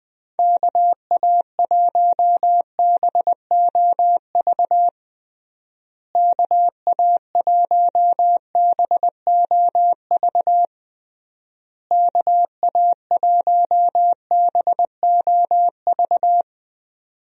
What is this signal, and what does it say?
Morse code, heard as a keyed tone, about 21 wpm, 700 Hz. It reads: KA1BOV KA1BOV KA1BOV
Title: USA Call Signs - 6 Characters - Repeated 3x - 20wpm